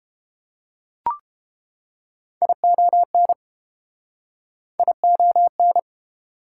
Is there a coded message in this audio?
Yes: Morse code from 2.4 to 5.8 s, heard as a keyed tone, about 31 wpm, 700 Hz. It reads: ION ION